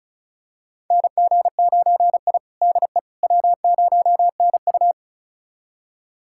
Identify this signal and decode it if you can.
Morse code, heard as a keyed tone, about 36 wpm, 700 Hz. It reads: NG9I DE W0NU
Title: USA Callsign DE Callsign - 4 Characters - 35wpm